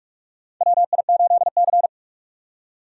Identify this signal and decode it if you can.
Morse code, heard as a keyed tone, about 47 wpm, 700 Hz. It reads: WI8C
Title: USA Call Signs - 4 Characters - 2x Character Spacing - 45wpm